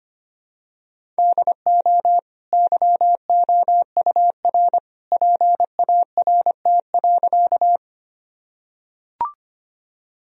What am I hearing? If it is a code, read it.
Morse code, heard as a keyed tone, about 25 wpm, 700 Hz. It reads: DO YOUR PART.
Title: Sentences from Top 1000 Words - Review Part 1 - 25wpm